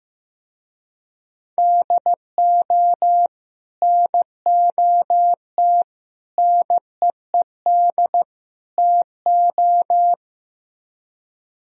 Morse code, heard as a keyed tone, about 16 wpm, 700 Hz. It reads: DO NOT NEED TO
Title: Speed-Racing - Sets of 4 Words 15wpm